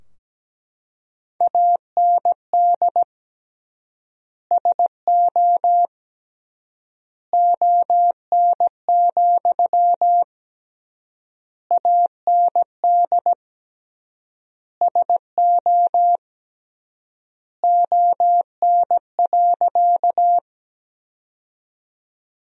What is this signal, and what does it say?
Morse code, heard as a keyed tone, about 17 wpm, 700 Hz. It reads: AND SO ON, AND SO ON.
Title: 3x Word-Spacing - Sentences from Top 100 Words - 17wpm